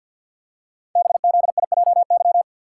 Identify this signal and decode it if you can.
Morse code, heard as a keyed tone, about 52 wpm, 700 Hz. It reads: BZIJY
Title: Code Groups - 5 Letters - 50wpm